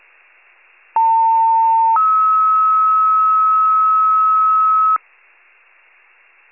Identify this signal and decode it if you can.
Two-tone page, heard as a one-second tone, then a three-second tone, about 900 then 1290 Hz. It is heard from 1.0 to 5.0 s.
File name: wcalert